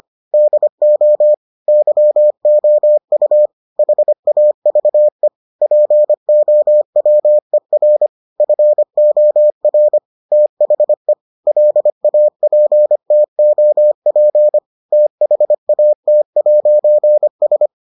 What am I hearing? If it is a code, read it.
Morse code, heard as a keyed tone, about 25 wpm, 600 Hz. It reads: DO YOU HAVE POWER FOR THE LAPTOP THAT'S